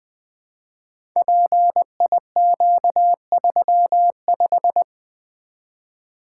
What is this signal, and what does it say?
Morse code, heard as a keyed tone, about 20 wpm, 700 Hz. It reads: PIQ35
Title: Code Groups - 5 Letter:Numbers - 20wpm